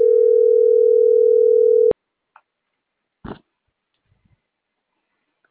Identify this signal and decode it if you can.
Ringback tone (North America), 440 + 480 Hz.